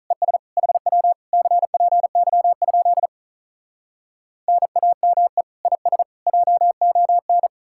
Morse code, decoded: ES HW CPY? NAME IS JON

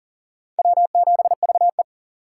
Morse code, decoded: W7VE

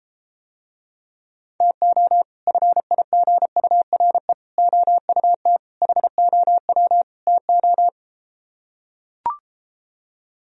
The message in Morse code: TO FIGURE OUT HOW TO

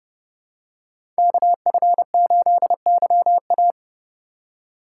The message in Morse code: KF8YA

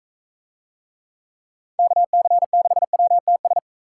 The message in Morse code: KC6WTS

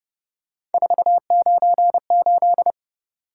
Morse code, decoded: 498

